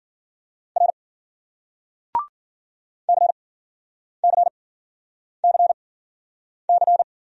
Morse code, heard as a keyed tone, about 53 wpm, spaced as for 9 wpm, 700 Hz. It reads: R CCCC